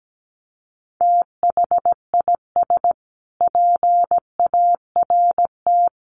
Morse code, heard as a keyed tone, about 17 wpm, 700 Hz. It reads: THIS PART
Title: Sets of 5 Words Encore2 17wpm